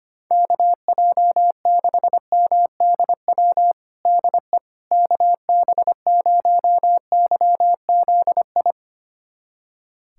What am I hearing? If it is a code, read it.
Morse code, heard as a keyed tone, about 25 wpm, 700 Hz. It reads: KJ6MDW DE KB0YZI
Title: USA Callsign DE Callsign 25wpm